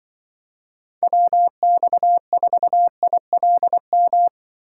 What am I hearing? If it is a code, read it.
Morse code, heard as a keyed tone, about 25 wpm, 700 Hz. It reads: WX4ILM
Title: Speed Racing - USA Call Signs 20wpm